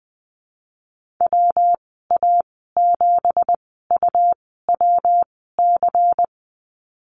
Morse code, heard as a keyed tone, about 20 wpm, spaced as for 15 wpm, 700 Hz. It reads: WA7UWC
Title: 167 Lesson 27g - Number 7 - Call Signs 20wpm